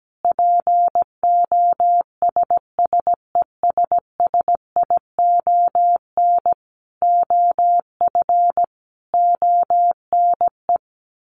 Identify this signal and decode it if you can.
Morse code, heard as a keyed tone, about 17 wpm, 700 Hz. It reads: POSSESSION OF ONE
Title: Sentences from Top 1400 Words 17wpm